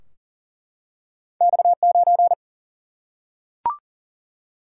Morse code, heard as a keyed tone, about 38 wpm, 700 Hz. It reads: X9
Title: Single-Letter Single-Number 40wpm